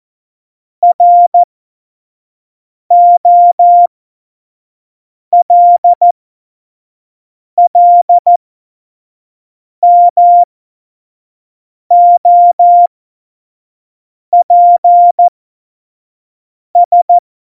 Morse code, heard as a keyed tone, about 14 wpm, spaced as for 5 wpm, 700 Hz. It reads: ROLLMOPS
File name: Morsekurs / Witze / Witz 11. Dezember 2018, 25 F70